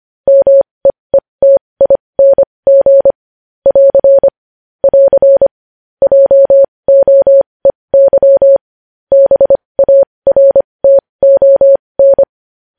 Morse code, heard as a keyed tone, about 25 wpm, spaced as for 20 wpm, 550 Hz. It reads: MEETING <AR> <AR> JOEY BARTON
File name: podcast / News headlines-1x-25x20